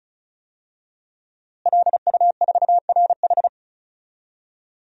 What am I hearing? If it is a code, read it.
Morse code, heard as a keyed tone, about 36 wpm, 700 Hz. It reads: LU4RH
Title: DXCC-100 Call Signs - 5 Characters - Repeated 2x - 35wpm